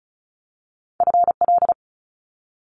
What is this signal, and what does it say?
Morse code, heard as a keyed tone, about 36 wpm, 700 Hz. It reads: FL
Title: Multi-letter Phonemes 35wpm